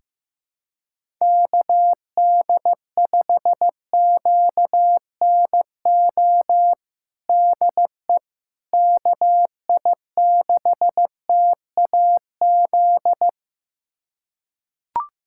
Morse code, heard as a keyed tone, about 15 wpm, 700 Hz. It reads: KD5QNO DE KI6TAZ
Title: USA Callsign DE Callsign 15wpm